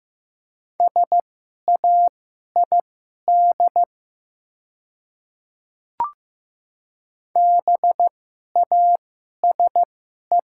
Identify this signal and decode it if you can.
Morse code, heard as a keyed tone, about 15 wpm, spaced as for 11 wpm, 700 Hz. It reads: SAID BASE
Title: 2x Character Spacing - Top 300 Words - 15wpm